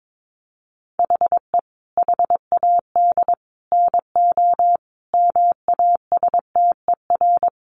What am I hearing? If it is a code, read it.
Morse code, heard as a keyed tone, about 22 wpm, 700 Hz. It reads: HE HAD NO MASTER
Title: Sentences from Top 1300 Words 22wpm